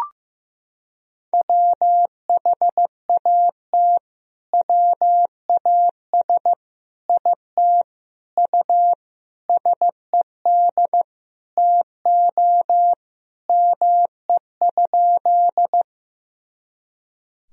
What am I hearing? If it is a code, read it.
Morse code, heard as a keyed tone, about 15 wpm, 700 Hz. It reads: WHAT WAS IT U SED TO ME?
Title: Sentences from Top 200 Words and Abbreviations 15wpm